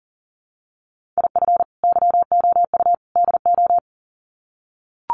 Morse code, heard as a keyed tone, about 40 wpm, 700 Hz. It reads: IF YOU DO
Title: Sets of 3 Words 40wpm